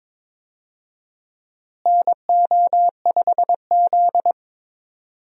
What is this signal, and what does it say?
Morse code, heard as a keyed tone, about 21 wpm, 700 Hz. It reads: NO5Z
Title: USA Call Signs - 4 Characters - Repeated 3x - 22wpm